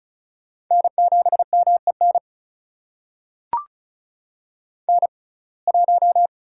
Morse code, heard as a keyed tone, about 36 wpm, spaced as for 25 wpm, 700 Hz. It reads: N7MEN N1
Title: USA Call Signs - 5 Characters - 6x Character Spacing - 35wpm